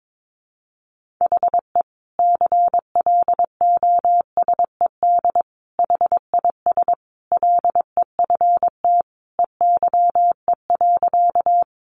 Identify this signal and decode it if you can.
Morse code, heard as a keyed tone, about 22 wpm, 700 Hz. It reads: HE CLOSED HIS LEFT EYE.
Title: Sentences from Top 1400 Words 22wpm